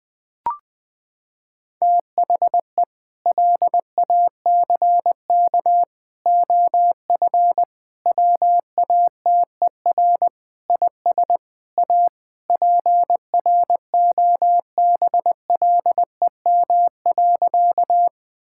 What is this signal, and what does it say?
Morse code, heard as a keyed tone, about 20 wpm, 700 Hz. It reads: THE LACK OF WATER IS A PROBLEM.
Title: Sentences with 3-Word Phrases - 20wpm